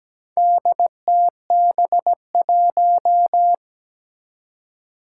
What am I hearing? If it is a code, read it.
Morse code, heard as a keyed tone, about 17 wpm, 700 Hz. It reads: DTB1
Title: Four Challenging Letter-Number Combinations 17wpm